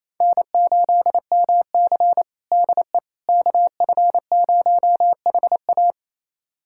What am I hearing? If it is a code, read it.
Morse code, heard as a keyed tone, about 28 wpm, 700 Hz. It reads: N8MC DE KF0HA
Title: USA Callsign DE Callsign 28wpm